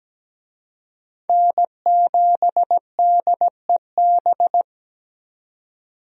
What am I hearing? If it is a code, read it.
Morse code, heard as a keyed tone, about 17 wpm, 700 Hz. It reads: N7DEB
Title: USA Call Signs - 5 Characters - 17wpm